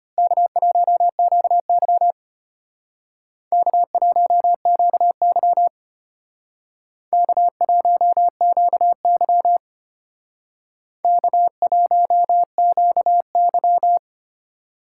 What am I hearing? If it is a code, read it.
Morse code, heard as a keyed tone, about 31 wpm, 700 Hz. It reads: K1QY K1QY K1QY K1QY